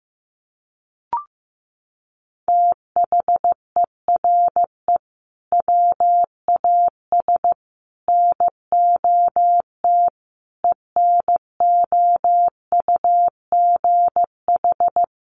Morse code, heard as a keyed tone, about 15 wpm, 700 Hz. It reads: THERE WAS NOT ENOUGH
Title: Sentences from Top 1400 Words 15wpm